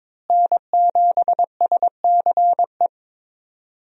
Morse code, heard as a keyed tone, about 22 wpm, 700 Hz. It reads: N7SCE
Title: USA Call Signs - 5 Characters - Repeated 3x - 22wpm